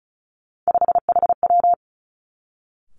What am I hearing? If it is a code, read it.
Morse code, heard as a keyed tone, about 36 wpm, 700 Hz. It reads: 5HW
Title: Three Challenging Letter-Number Combinations 35wpm